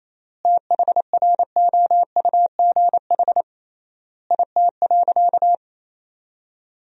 Morse code, heard as a keyed tone, about 28 wpm, 700 Hz. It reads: THROUGH IT.
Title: 3x Word-Spacing - Sentences from Top 200 Words - 28wpm